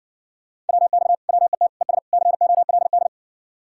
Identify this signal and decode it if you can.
Morse code, heard as a keyed tone, about 60 wpm, 700 Hz. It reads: WX WET ES COLD